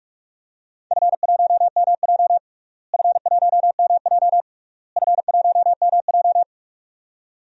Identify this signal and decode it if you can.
Morse code, heard as a keyed tone, about 46 wpm, 700 Hz. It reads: F1MJ F1MJ F1MJ
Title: DXCC-100 Call Signs - 4 Characters - Repeated 3x - 45wpm